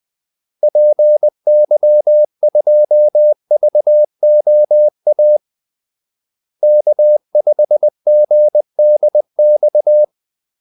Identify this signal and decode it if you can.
Morse code, heard as a keyed tone, about 20 wpm, 600 Hz. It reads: PY2VOA K5GDX
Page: Kaldesignaler 20 wpm